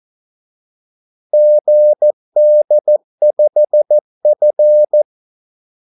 Morse code, heard as a keyed tone, about 14 wpm, 600 Hz. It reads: GD5F
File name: Callsigns-14wpm